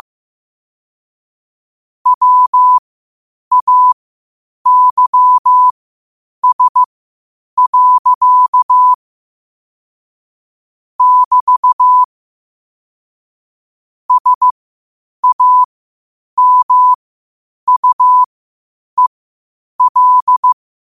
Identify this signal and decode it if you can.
Morse code, heard as a keyed tone, about 15 wpm, spaced as for 8 wpm, 1000 Hz. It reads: WAYS. <BT> SAMUEL
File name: qotdfiles / 8wpm